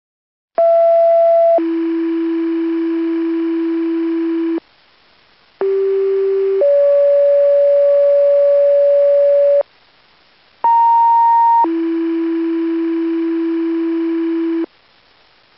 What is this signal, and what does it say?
Two-tone page, heard as a one-second tone, then a three-second tone, about 670 then 330 Hz, about 390 then 570 Hz, about 930 then 330 Hz, with noise. It is heard from 0.6 to 14.7 s.